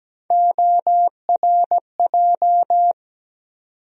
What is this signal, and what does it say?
Morse code, heard as a keyed tone, about 17 wpm, 700 Hz. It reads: ORJ
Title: USA Call Sign Postfixes - 3 Letters 17wpm